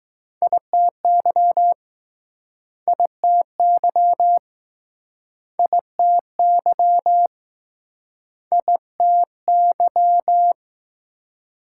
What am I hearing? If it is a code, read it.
Morse code, heard as a keyed tone, about 19 wpm, 700 Hz. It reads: ITY ITY ITY ITY